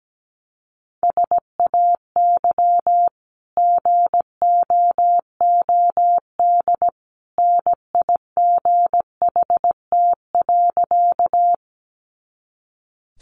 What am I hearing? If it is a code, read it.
Morse code, heard as a keyed tone, about 17 wpm, 700 Hz. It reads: SAY GOOD NIGHT.
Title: Sentences from Top 300 Words 17wpm